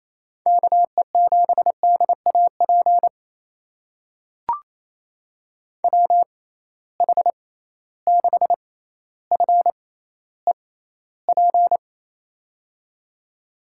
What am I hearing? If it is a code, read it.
Morse code, heard as a keyed tone, about 28 wpm, spaced as for 14 wpm, 700 Hz. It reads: KE7DAP WH6FEP
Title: USA Call Signs - 6 Characters - 6x Character Spacing - 28wpm